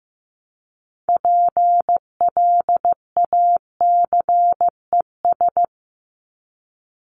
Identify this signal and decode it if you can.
Morse code, heard as a keyed tone, about 15 wpm, 700 Hz. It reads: PLACES
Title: ES:S Suffix - Top 100 Words with Base Words 15wpm